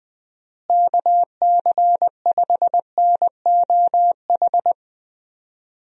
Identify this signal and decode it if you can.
Morse code, heard as a keyed tone, about 20 wpm, 700 Hz. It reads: KC5NOH
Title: USA Call Signs - 6 Characters - Repeated 3x - 20wpm